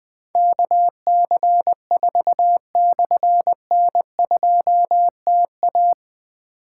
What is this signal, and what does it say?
Morse code, heard as a keyed tone, about 20 wpm, 700 Hz. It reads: KC4/N2TA
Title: Difficult Call Signs 20wpm